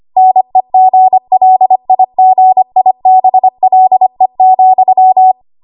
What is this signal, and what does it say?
Morse code, heard as a keyed tone, about 25 wpm, 750 Hz. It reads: NEGLIGIBLE,
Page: W1AW Web Code Practice Files